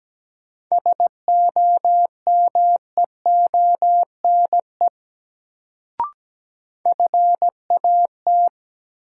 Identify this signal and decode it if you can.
Morse code, heard as a keyed tone, about 17 wpm, 700 Hz. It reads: SOMEONE FAT
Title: Sentences from Top 600 Words - New Word Drill - 17wpm